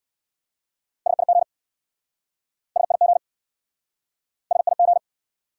Morse code, heard as a keyed tone, about 64 wpm, 700 Hz. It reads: SIG SIG SIG